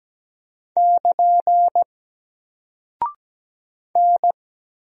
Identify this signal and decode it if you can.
Morse code, heard as a keyed tone, about 17 wpm, 700 Hz. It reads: <KN> N
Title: Single Character (Letter:Number:Punctuation:Prosign) 17wpm